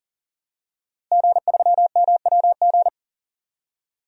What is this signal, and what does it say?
Morse code, heard as a keyed tone, about 40 wpm, 700 Hz. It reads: G3MWG
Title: DXCC-100 Call Signs - 5 Characters - Repeated 3x - 40wpm